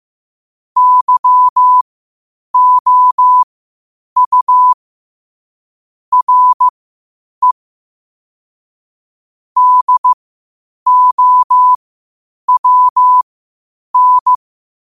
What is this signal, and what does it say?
Morse code, heard as a keyed tone, about 15 wpm, spaced as for 8 wpm, 1000 Hz. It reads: YOU RE DOWN
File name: qotdfiles / 8wpm